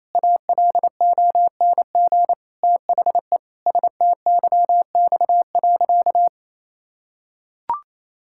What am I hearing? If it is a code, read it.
Morse code, heard as a keyed tone, about 28 wpm, 700 Hz. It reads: ALONG THE STYX.
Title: Pangram Challenge v2 - 28wpm